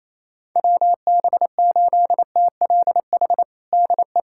Morse code, heard as a keyed tone, about 28 wpm, 700 Hz. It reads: WB8TLH DE
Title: USA Callsign DE Callsign - 6 Characters - 28wpm